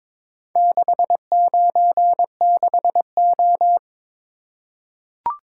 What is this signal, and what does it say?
Morse code, heard as a keyed tone, about 22 wpm, 700 Hz. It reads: 696O
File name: practice / 22wpm / Four Letter-Number Combinations - 22wpm